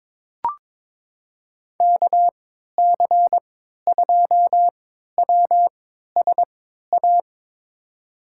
Morse code, heard as a keyed tone, about 22 wpm, spaced as for 12 wpm, 700 Hz. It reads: KC2WSA
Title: USA Call Signs - 3x Character Spacing - 22wpm